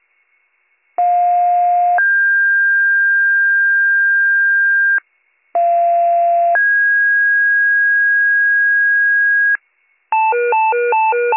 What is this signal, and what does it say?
Two-tone page, heard as a one-second tone, then a three-second tone, about 710 then 1600 Hz, about 690 then 1650 Hz. It is heard from 1.0 to 9.6 s.